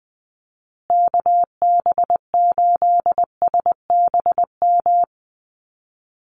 Morse code, heard as a keyed tone, about 20 wpm, 700 Hz. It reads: KB8SBM
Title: USA Call Signs Repeated 2x 20wpm